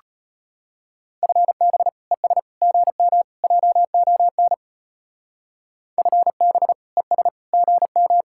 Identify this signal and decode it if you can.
Morse code, heard as a keyed tone, about 32 wpm, 700 Hz. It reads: FB ES GM JON FB ES GM